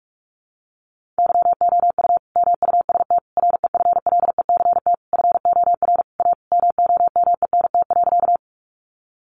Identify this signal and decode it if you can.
Morse code, heard as a keyed tone, about 44 wpm, 700 Hz. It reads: YOU MUST REFLECT FOR A MOMENT.
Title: Sentences from Top 1500 Words 45wpm